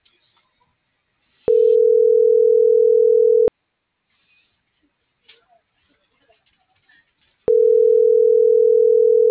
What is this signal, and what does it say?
Ringback tone (North America), 440 + 480 Hz.